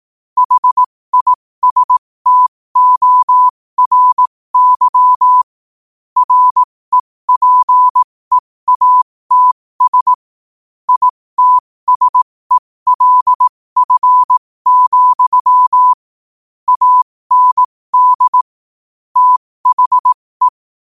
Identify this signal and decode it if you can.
Morse code, heard as a keyed tone, about 18 wpm, spaced as for 15 wpm, 1000 Hz. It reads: HISTORY REPEATS ITSELF, AND THE
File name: qotdfiles / 15wpm